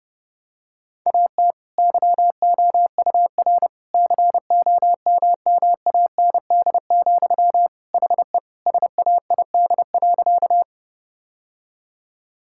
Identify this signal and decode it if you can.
Morse code, heard as a keyed tone, about 30 wpm, 700 Hz. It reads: AT YOUR COMMAND, HE SAID.
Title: Sentences from Top 1600 Words 30wpm